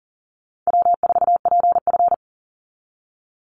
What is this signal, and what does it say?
Morse code, heard as a keyed tone, about 40 wpm, 700 Hz. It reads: W4PF
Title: Easy Call Signs - 40wpm